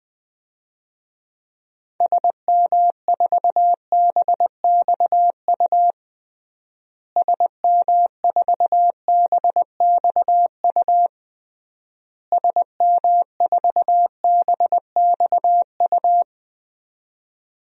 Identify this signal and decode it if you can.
Morse code, heard as a keyed tone, about 20 wpm, 700 Hz. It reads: SM4BXU SM4BXU SM4BXU
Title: DXCC-100 Call Signs - 6 Characters - Repeated 3x - 20wpm